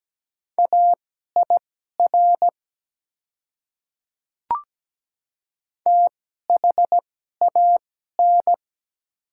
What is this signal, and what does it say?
Morse code, heard as a keyed tone, about 17 wpm, spaced as for 12 wpm, 700 Hz. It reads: AIR THAN
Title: 2x Character Spacing - Top 200 Words - 17wpm